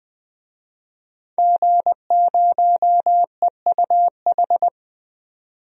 Morse code, heard as a keyed tone, about 20 wpm, 700 Hz. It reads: G0EUH
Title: DXCC-100 Call Signs - 5 Characters - 20wpm